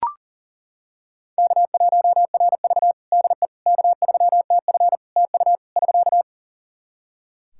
Morse code, heard as a keyed tone, about 40 wpm, 700 Hz. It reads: K1RV DE K3TF TU <SK>
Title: Practice QSO - Exchange 4 - Ending - 40wpm